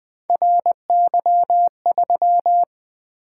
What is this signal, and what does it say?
Morse code, heard as a keyed tone, about 20 wpm, 700 Hz. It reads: RY3
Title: Three Letter-Number Combinations - 20wpm